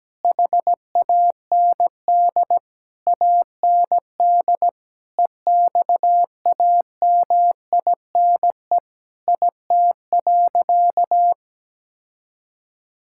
Morse code, heard as a keyed tone, about 17 wpm, 700 Hz. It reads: HAND AND EXAMINE IT.
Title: Sentences from Top 1600 Words 17wpm